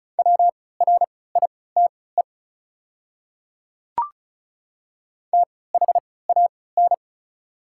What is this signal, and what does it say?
Morse code, heard as a keyed tone, about 34 wpm, spaced as for 20 wpm, 700 Hz. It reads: WRITE THAN
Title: 3x Character Spacing - Top 100 Words - 35wpm